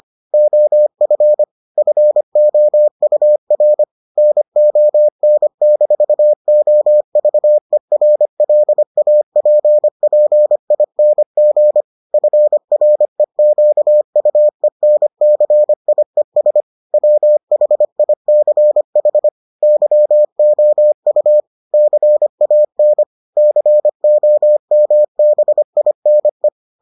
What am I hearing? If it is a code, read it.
Morse code, heard as a keyed tone, about 25 wpm, 600 Hz. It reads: OF FOUR NON-OVERLAPPING FREQUENCIES WHICH YOU CAN COMBINE